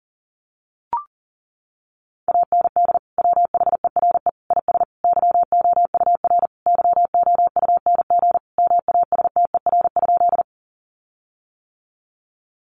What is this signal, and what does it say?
Morse code, heard as a keyed tone, about 40 wpm, 700 Hz. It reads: AND WHERE IS YOUR YOUNG MASTER?
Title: Sentences from Top 1300 Words 40wpm